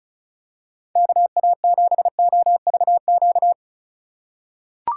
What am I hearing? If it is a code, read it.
Morse code, heard as a keyed tone, about 34 wpm, 700 Hz. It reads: KA7OVQ
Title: USA Call Signs - 6 Characters - 2x Character Spacing - 35wpm